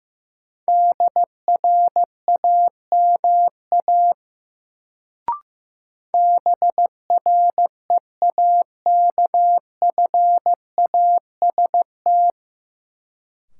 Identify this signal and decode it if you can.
Morse code, heard as a keyed tone, about 16 wpm, 700 Hz. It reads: DRAMA BREAKFAST
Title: Sentences from Top 1500 Words - New Word Drill Alt - 15wpm